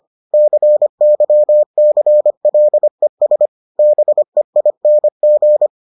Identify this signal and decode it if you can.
Morse code, heard as a keyed tone, about 25 wpm, 600 Hz. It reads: CYCLES BEING